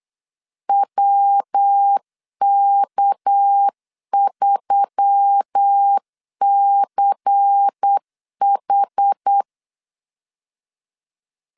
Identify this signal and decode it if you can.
Morse code, heard as a keyed tone, about 8 wpm, 800 Hz. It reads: WK3CH
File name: LEZIONI / 6 PKX0 / 6 PK 8wpm 3003